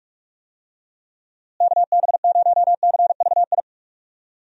Morse code, heard as a keyed tone, about 46 wpm, 700 Hz. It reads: KB0CVI
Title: USA Call Signs - 6 Characters - Repeated 2x - 45wpm